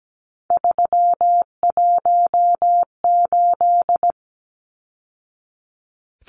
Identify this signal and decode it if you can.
Morse code, heard as a keyed tone, about 17 wpm, 700 Hz. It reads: 318
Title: Numbers - 3-Digit 17wpm